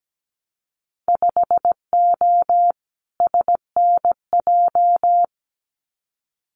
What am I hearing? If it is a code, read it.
Morse code, heard as a keyed tone, about 17 wpm, 700 Hz. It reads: 5O SNJ
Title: Winter Field Day Exchanges Encore 17wpm